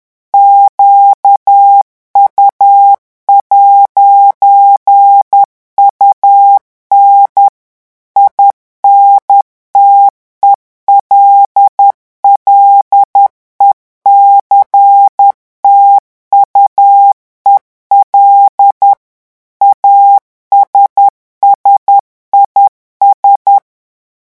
Morse code, heard as a keyed tone, about 10 wpm, 800 Hz. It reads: QU'UN INTELLECTUEL ASSIS
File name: morse mp3 / revision francais 10 mots minute